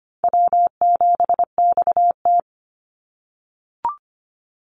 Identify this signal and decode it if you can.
Morse code, heard as a keyed tone, about 25 wpm, 700 Hz. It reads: W7XT
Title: Easy Call Signs - 25wpm